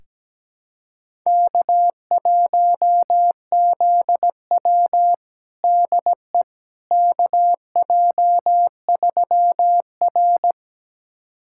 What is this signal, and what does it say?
Morse code, heard as a keyed tone, about 17 wpm, 700 Hz. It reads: K1ZW DE KJ3R
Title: USA Callsign DE Callsign - 4 Characters - 17wpm